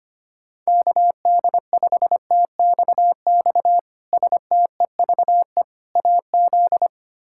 Morse code, heard as a keyed tone, about 25 wpm, 700 Hz. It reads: KD5TXX STEVE AZ
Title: K1USN SST QSO Practice - 25wpm